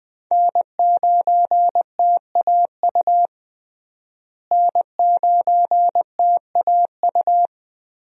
Morse code, heard as a keyed tone, about 20 wpm, 700 Hz. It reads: N9TAU N9TAU
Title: USA Call Signs - 5 Characters - Repeated 2x - 20wpm